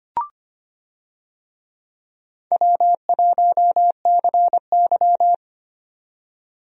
Morse code, heard as a keyed tone, about 25 wpm, 700 Hz. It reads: W1CY
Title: USA Call Signs - 4 Characters - 25wpm